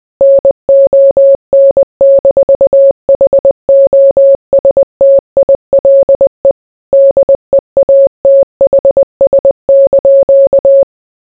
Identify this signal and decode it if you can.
Morse code, heard as a keyed tone, about 20 wpm, 550 Hz. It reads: NON-HOSTILE DEATHS)